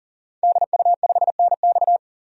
Morse code, heard as a keyed tone, about 37 wpm, 700 Hz. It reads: DU5NX
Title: DXCC-100 Call Signs - 5 Characters - 40wpm